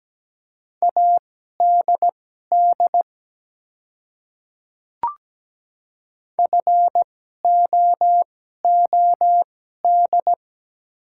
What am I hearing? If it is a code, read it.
Morse code, heard as a keyed tone, about 17 wpm, spaced as for 12 wpm, 700 Hz. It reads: ADD FOOD
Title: 2x Character Spacing - Top 300 Words - 17wpm